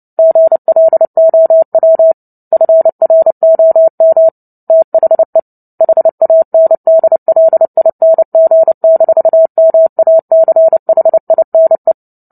Morse code, heard as a keyed tone, about 30 wpm, 650 Hz. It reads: GLOW FROM THE HANDLING-MACHINE
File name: Book Two / 03-Chapter 3 - 29 WPM